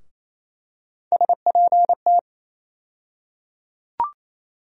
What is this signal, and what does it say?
Morse code, heard as a keyed tone, about 28 wpm, 700 Hz. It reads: SPT
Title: USA Call Sign Postfixes - 3 Letters 28wpm